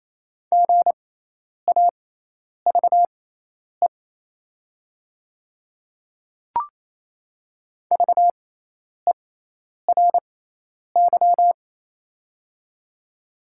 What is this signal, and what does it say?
Morse code, heard as a keyed tone, about 28 wpm, spaced as for 10 wpm, 700 Hz. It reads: GAVE VERY
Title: 6x Character Spacing - Top 500 Words - 28wpm